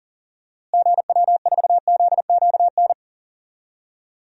Morse code, heard as a keyed tone, about 37 wpm, 700 Hz. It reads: GW4ZQN
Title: DXCC-100 Call Signs - 6 Characters - Repeated 3x - 40wpm